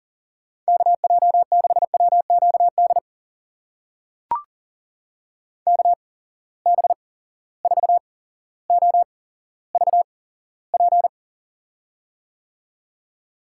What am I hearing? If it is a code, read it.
Morse code, heard as a keyed tone, about 38 wpm, spaced as for 17 wpm, 700 Hz. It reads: KJ6WQD KB4OVP
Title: USA Call Signs - 6 Characters - 8x Character Spacing - 40wpm